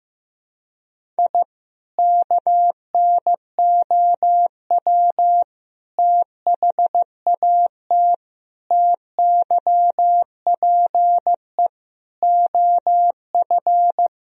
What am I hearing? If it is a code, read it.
Morse code, heard as a keyed tone, about 15 wpm, 700 Hz. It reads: I KNOW THAT TYPE OF
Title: Sentences from Top 1000 Words - Review Part 3 - 15wpm